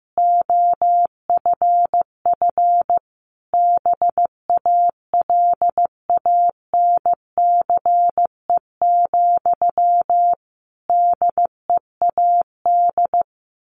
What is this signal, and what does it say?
Morse code, heard as a keyed tone, about 15 wpm, 700 Hz. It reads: OFF BALANCE, DEAD
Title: A Sound Of Thunder 15wpm